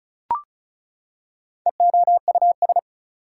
Morse code, heard as a keyed tone, about 34 wpm, 700 Hz. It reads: EOUS
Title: Prefixes and Suffixes 35wpm